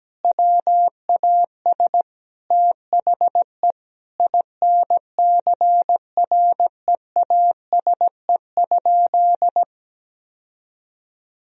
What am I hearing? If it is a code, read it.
Morse code, heard as a keyed tone, about 17 wpm, 700 Hz. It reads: WAS THE INCREASE?
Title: Sentences from Top 1000 Words - Review Part 4 - 17wpm